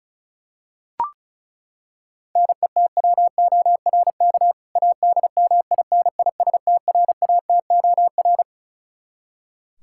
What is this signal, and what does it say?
Morse code, heard as a keyed tone, about 34 wpm, 700 Hz. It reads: NETWORK ADMINISTRATOR
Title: K1USN SST QSO Practice 35wpm